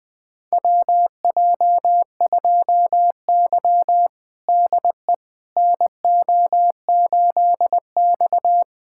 Morse code, heard as a keyed tone, about 20 wpm, 700 Hz. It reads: WJ2Y DE NO8X
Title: USA Callsign DE Callsign - 4 Characters - 20wpm